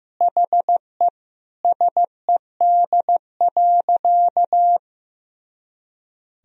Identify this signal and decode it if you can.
Morse code, heard as a keyed tone, about 15 wpm, 700 Hz. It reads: HE SED.
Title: Sentences from Top 300 Words and Abbreviations 15wpm